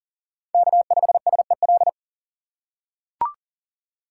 Morse code, heard as a keyed tone, about 37 wpm, 700 Hz. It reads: K5SEL
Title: USA Call Signs - 5 Characters - 2x Character Spacing - 40wpm